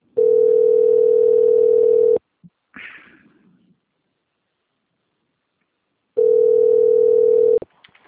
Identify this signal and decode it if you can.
Ringback tone (North America), 440 + 480 Hz.